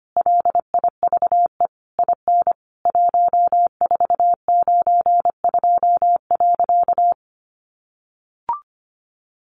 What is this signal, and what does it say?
Morse code, heard as a keyed tone, about 25 wpm, 700 Hz. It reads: LIVE IN 1492.
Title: A Sound Of Thunder 25wpm